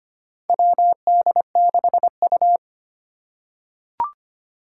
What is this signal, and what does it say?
Morse code, heard as a keyed tone, about 25 wpm, 700 Hz. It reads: WD6U
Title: USA Call Signs - 4 Characters - 4x Character Spacing - 25wpm